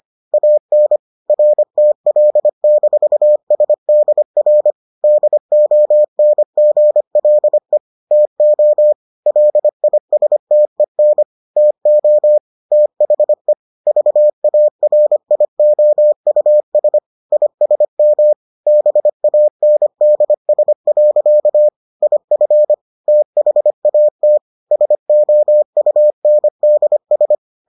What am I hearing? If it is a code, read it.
Morse code, heard as a keyed tone, about 25 wpm, 600 Hz. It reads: AN RTL-SDR DONGLE TO LISTEN TO THE VARIOUS ISM BANDS. IF THAT SOUNDS